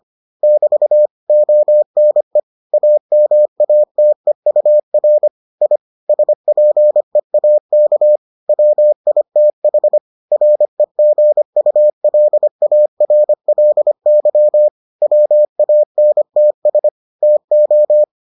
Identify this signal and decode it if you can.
Morse code, heard as a keyed tone, about 25 wpm, 600 Hz. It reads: <BT> ONE AMATEUR I SPEAK WITH REGULARLY WANTS TO